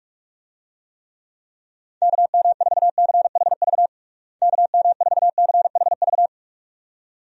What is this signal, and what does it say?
Morse code, heard as a keyed tone, about 46 wpm, 700 Hz. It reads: KM4CHV KM4CHV
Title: USA Call Signs - 6 Characters - Repeated 2x - 45wpm